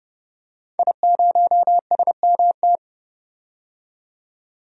Morse code, heard as a keyed tone, about 32 wpm, 700 Hz. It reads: I0SMT
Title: DXCC-100 Call Signs - 5 Characters - 30wpm